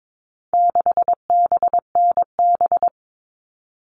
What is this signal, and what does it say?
Morse code, heard as a keyed tone, about 22 wpm, 700 Hz. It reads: 6BNB
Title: All Cut Numbers - 4-Digit 22wpm